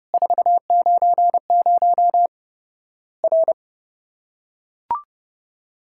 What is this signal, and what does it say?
Morse code, heard as a keyed tone, about 30 wpm, 700 Hz. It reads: 490 R
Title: Worked All Europe DX Contest - QTC Exchange - 30wpm